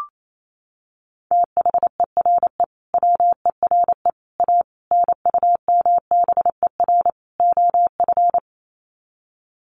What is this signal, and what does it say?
Morse code, heard as a keyed tone, about 28 wpm, 700 Hz. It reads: THERE WERE A NUMBER OF
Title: Sets of 5 Words Encore2 28wpm